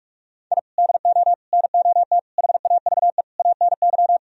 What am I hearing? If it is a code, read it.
Morse code, heard as a keyed tone, about 46 wpm, 700 Hz. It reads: I DO NOT HAVE ANY
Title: Sentences from Top 1000 Words - Review Part 4 - 45wpm